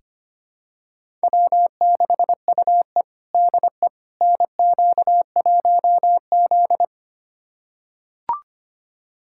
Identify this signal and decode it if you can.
Morse code, heard as a keyed tone, about 25 wpm, 700 Hz. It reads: W6UE DE NQ1Z
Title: USA Callsign DE Callsign - 4 Characters - 25wpm